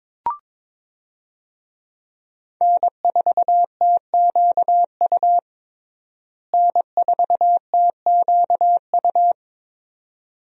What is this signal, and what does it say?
Morse code, heard as a keyed tone, about 22 wpm, 700 Hz. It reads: N4TQU N4TQU
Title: USA Call Signs - 5 Characters - Repeated 2x - 22wpm